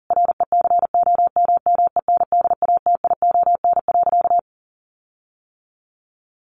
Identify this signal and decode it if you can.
Morse code, heard as a keyed tone, about 40 wpm, 700 Hz. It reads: RECOMMENDATION.
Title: Sentences from Top 1400 Words 40wpm